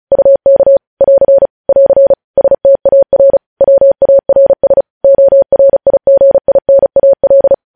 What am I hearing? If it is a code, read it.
Morse code, heard as a keyed tone, about 36 wpm, 550 Hz. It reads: UK <AR> <AR> STAR WARS ORIGINAL
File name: podcast / News headlines-1x-35x35